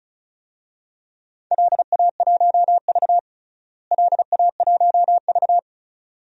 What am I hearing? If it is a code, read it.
Morse code, heard as a keyed tone, about 36 wpm, 700 Hz. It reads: LA1V LA1V
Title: DXCC-100 Call Signs - 4 Characters - Repeated 2x - 35wpm